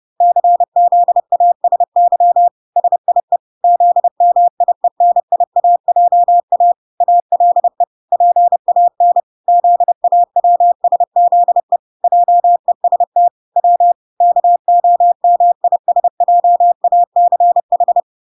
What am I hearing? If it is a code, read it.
Morse code, heard as a keyed tone, about 30 wpm, 700 Hz. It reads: CZASY SIE ZMIENIAJA ALE PAN ZAWSZE JEST W KOMISJACH